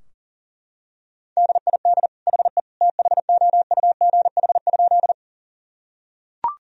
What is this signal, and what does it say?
Morse code, heard as a keyed tone, about 40 wpm, 700 Hz. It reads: DID HE THOUGH?
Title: Sentences from Top 1000 Words - Review Part 2 - 40wpm